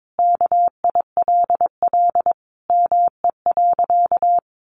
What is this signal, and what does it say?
Morse code, heard as a keyed tone, about 22 wpm, 700 Hz. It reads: KILL ME.
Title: Sentences from Top 1400 Words 22wpm